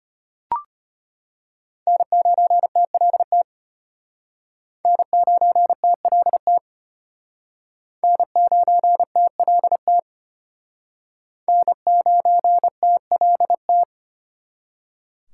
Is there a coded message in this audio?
Yes: Morse code from 1.9 to 13.8 s, heard as a keyed tone, about 32 wpm, 700 Hz. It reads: N9TLT N9TLT N9TLT N9TLT